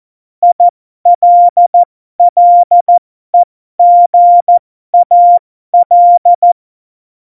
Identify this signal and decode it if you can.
Morse code, heard as a keyed tone, about 14 wpm, spaced as for 12 wpm, 700 Hz. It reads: ILLEGAL
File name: Morsekurs / Witze